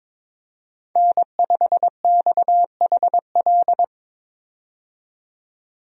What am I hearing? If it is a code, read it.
Morse code, heard as a keyed tone, about 22 wpm, 700 Hz. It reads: N5XHL
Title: USA Call Signs - 5 Characters - 22wpm